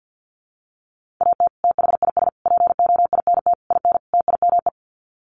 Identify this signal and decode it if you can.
Morse code, heard as a keyed tone, about 50 wpm, 700 Hz. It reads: AT THIS POINT IN TIME
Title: Sets of 5 Words Encore 50wpm